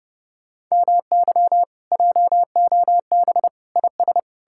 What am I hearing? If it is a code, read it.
Morse code, heard as a keyed tone, about 37 wpm, 700 Hz. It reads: MY JOB IS
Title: Speed-Racing Common QSO Phrases - 30wpm